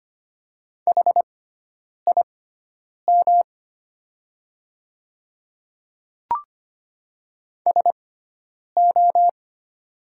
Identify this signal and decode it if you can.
Morse code, heard as a keyed tone, about 25 wpm, spaced as for 9 wpm, 690 Hz. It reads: HIM SO
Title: 6x Character Spacing - Top 200 Words - 25wpm